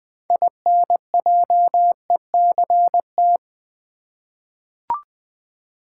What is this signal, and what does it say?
Morse code, heard as a keyed tone, about 20 wpm, 700 Hz. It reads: INJECT
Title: 6 Letter Words - First 3 Letters Given - 20wpm